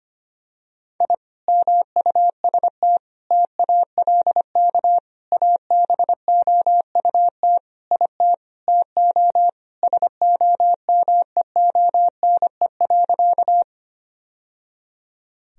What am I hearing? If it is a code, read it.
Morse code, heard as a keyed tone, about 25 wpm, 700 Hz. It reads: I MUST TALK ABOUT IT TO SOMEONE.
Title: Sentences from Top 1000 Words - Review Part 4 - 25wpm